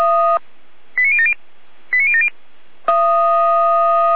Five-tone selective call (ZVEI), heard as quick runs of five tones, tones of about 70 ms. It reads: 89070 89070